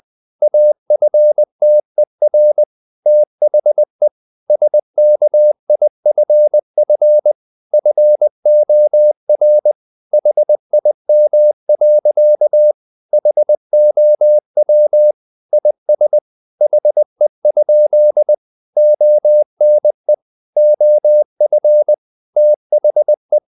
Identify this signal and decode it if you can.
Morse code, heard as a keyed tone, about 20 wpm, 600 Hz. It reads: AFTER THE SKIFF FOR HIM. HOW IS HE? ONE OF THE
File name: the old man and the sea part 2 20wpm